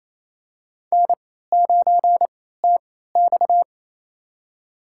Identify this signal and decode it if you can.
Morse code, heard as a keyed tone, about 28 wpm, spaced as for 16 wpm, 700 Hz. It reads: N9TX
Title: USA Call Signs - 4 Characters - 3x Character Spacing - 28wpm